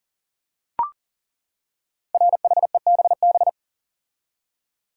Morse code, heard as a keyed tone, about 40 wpm, 700 Hz. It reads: RHEBB
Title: Code Groups - 5 Letters - 40wpm